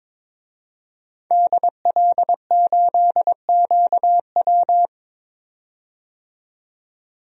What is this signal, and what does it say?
Morse code, heard as a keyed tone, about 22 wpm, 700 Hz. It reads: DL8QW
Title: DXCC-100 Call Signs - 5 Characters - 22wpm